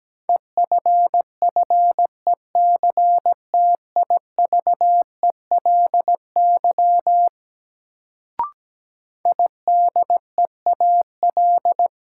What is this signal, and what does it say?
Morse code, heard as a keyed tone, about 17 wpm, 700 Hz. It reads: EFFECTIVELY IDEAL
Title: Sentences from Top 1400 Words - New Word Drill Alt - 17wpm